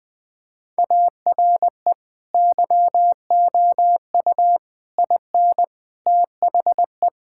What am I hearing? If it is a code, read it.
Morse code, heard as a keyed tone, about 21 wpm, 700 Hz. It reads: ARE YOU IN THE
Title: Sentences from Top 1000 Words - Review Part 6 - 20wpm